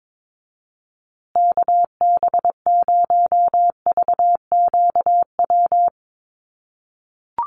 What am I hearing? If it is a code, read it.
Morse code, heard as a keyed tone, about 22 wpm, 700 Hz. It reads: KB0VQW
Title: USA Call Signs Repeated 2x 22wpm